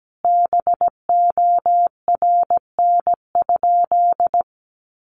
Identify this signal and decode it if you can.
Morse code, heard as a keyed tone, about 17 wpm, 700 Hz. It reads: BORN?
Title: Sentences from Top 1000 Words 17wpm